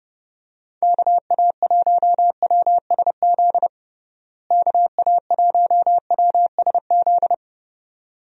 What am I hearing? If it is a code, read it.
Morse code, heard as a keyed tone, about 30 wpm, 700 Hz. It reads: KA1WSZ KA1WSZ
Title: USA Call Signs - 6 Characters - Repeated 2x - 30wpm